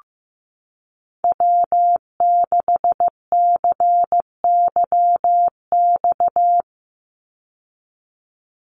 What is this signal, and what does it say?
Morse code, heard as a keyed tone, about 15 wpm, 700 Hz. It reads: W6CYX
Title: Call Signs Repeated 3x - 15wpm